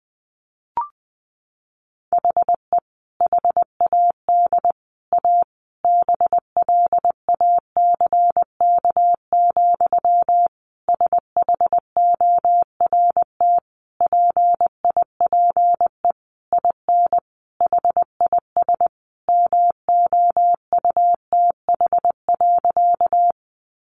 Morse code, heard as a keyed tone, about 20 wpm, 700 Hz. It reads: HE HAD A BLACK, SHORT PIPE IN HIS MOUTH.
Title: Sentences from Top 1500 Words 20wpm